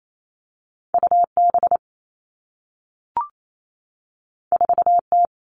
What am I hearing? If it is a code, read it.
Morse code, heard as a keyed tone, about 28 wpm, 700 Hz. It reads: U B 4 T
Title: Two Challenging Letter-Number Combinations 28wpm